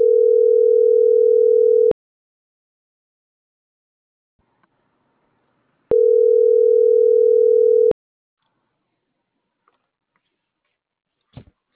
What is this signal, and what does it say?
Ringback tone (North America), 440 + 480 Hz.